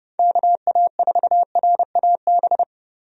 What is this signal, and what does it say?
Morse code, heard as a keyed tone, about 29 wpm, 700 Hz. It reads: KA4RAB
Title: USA Call Signs - 6 Characters - Repeated 2x - 30wpm